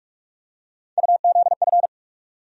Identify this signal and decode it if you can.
Morse code, heard as a keyed tone, about 46 wpm, 700 Hz. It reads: UZF
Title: USA Call Sign Postfixes - 3 Letters 45wpm